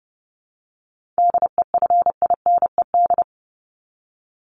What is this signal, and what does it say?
Morse code, heard as a keyed tone, about 30 wpm, 700 Hz. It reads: DEFINED
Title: ED Suffix - Top 100 Words with Base Words 30wpm